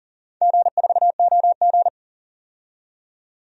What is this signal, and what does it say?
Morse code, heard as a keyed tone, about 40 wpm, 700 Hz. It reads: G4OG
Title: DXCC-100 Call Signs - 4 Characters - 40wpm